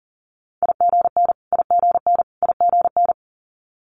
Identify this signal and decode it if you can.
Morse code, heard as a keyed tone, about 39 wpm, 700 Hz. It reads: IGN IGN IGN
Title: Multi-letter Phonemes Repeated 3x 40wpm